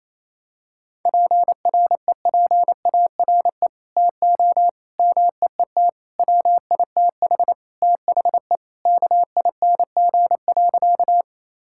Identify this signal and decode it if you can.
Morse code, heard as a keyed tone, about 28 wpm, 700 Hz. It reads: PREPARE TO MEET WITH THE KING.
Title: Sentences from Top 1000 Words - Review Part 6 - 28wpm